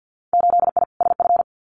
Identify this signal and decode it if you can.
Morse code, heard as a keyed tone, about 50 wpm, 700 Hz. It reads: 8I SF
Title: Winter Field Day Exchanges Encore 50wpm